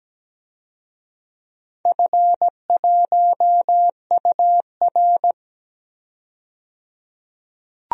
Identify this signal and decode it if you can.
Morse code, heard as a keyed tone, about 17 wpm, 700 Hz. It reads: F1UR
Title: DXCC-100 Call Signs - 4 Characters - 17wpm